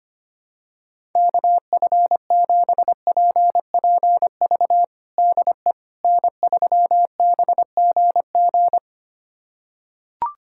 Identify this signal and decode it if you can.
Morse code, heard as a keyed tone, about 25 wpm, 700 Hz. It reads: KF7PPV DE N3BGG
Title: USA Callsign DE Callsign 25wpm